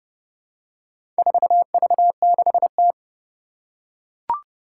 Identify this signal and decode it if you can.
Morse code, heard as a keyed tone, about 30 wpm, 690 Hz. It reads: 4V6T
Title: Four Challenging Letter-Number Combinations 30wpm